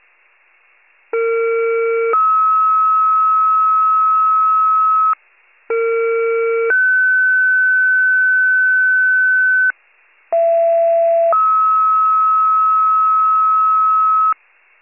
Two-tone page, heard as a one-second tone, then a three-second tone, about 460 then 1290 Hz, about 460 then 1500 Hz, about 670 then 1290 Hz, with noise. It is heard from 1.1 to 14.3 s.